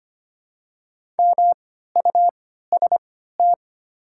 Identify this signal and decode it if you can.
Morse code, heard as a keyed tone, about 25 wpm, spaced as for 14 wpm, 700 Hz. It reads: MUST
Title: 3x Character Spacing - Top 200 Words - 25wpm